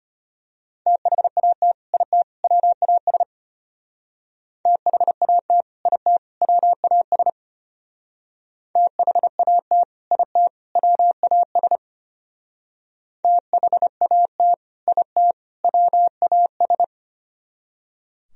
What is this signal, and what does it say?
Morse code, heard as a keyed tone, about 32 wpm, 700 Hz. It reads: THAT IT WAS THAT IT WAS THAT IT WAS THAT IT WAS